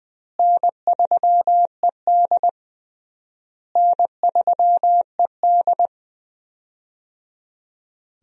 Morse code, heard as a keyed tone, about 20 wpm, 700 Hz. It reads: N3ED N3ED
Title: USA Call Signs - 4 Characters - Repeated 3x - 20wpm